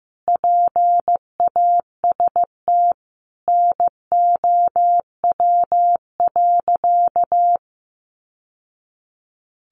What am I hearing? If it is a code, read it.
Morse code, heard as a keyed tone, about 14 wpm, 700 Hz. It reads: PAST NOW.
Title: Sentences from Top 600 Words 15wpm